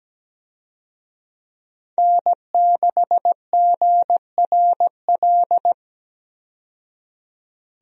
Morse code, heard as a keyed tone, about 17 wpm, 700 Hz. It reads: N6GRL
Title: USA Call Signs - 5 Characters - 17wpm